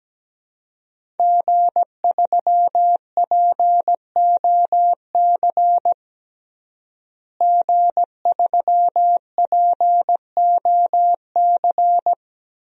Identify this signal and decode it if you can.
Morse code, heard as a keyed tone, about 17 wpm, 700 Hz. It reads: G3POC G3POC
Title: DXCC-100 Call Signs - 5 Characters - Repeated 3x - 17wpm